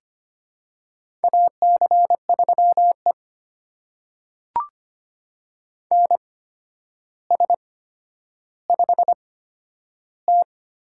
Morse code, heard as a keyed tone, about 25 wpm, spaced as for 11 wpm, 700 Hz. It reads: AC3E NS5T
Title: USA Call Signs - 4 Characters - 8x Character Spacing - 25wpm